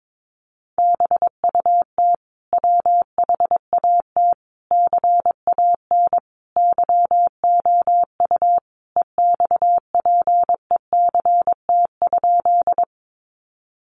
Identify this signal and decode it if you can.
Morse code, heard as a keyed tone, about 22 wpm, 700 Hz. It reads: BUT WHAT CAN YOU EXPECT?
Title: Sentences from Top 1100 Words 22wpm